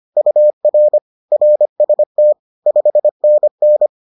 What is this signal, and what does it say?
Morse code, heard as a keyed tone, about 25 wpm, 600 Hz. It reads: UR RST 5NN
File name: 25wpm-6